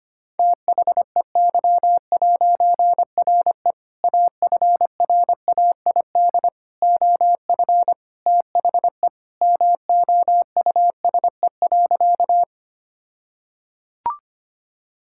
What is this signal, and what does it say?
Morse code, heard as a keyed tone, about 25 wpm, 700 Hz. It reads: THEY'RE AFRAID OF THE MOUSE.
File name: practice / 25wpm / All-Age Appropriate Jokes - 25wpm